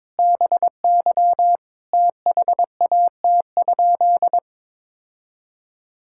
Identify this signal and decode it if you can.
Morse code, heard as a keyed tone, about 22 wpm, 700 Hz. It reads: BY THAT?
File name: practice / 22wpm / Sentences with 4-Word Phrases - 22wpm